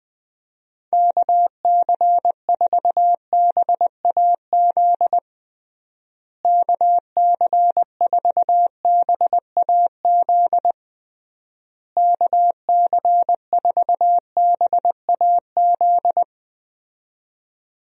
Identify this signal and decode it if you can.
Morse code, heard as a keyed tone, about 19 wpm, 700 Hz. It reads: KC4BAZ KC4BAZ KC4BAZ
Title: USA Call Signs - 6 Characters - Repeated 3x - 20wpm